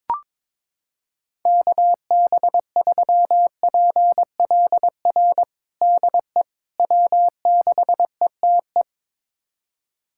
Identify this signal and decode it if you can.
Morse code, heard as a keyed tone, about 22 wpm, 700 Hz. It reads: KB3PLR DE W6ETE
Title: USA Callsign DE Callsign 22wpm